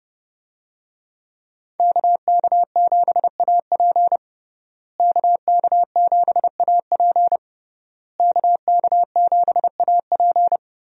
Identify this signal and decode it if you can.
Morse code, heard as a keyed tone, about 30 wpm, 700 Hz. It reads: KK7AP KK7AP KK7AP
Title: USA Call Signs - 5 Characters - Repeated 3x - 30wpm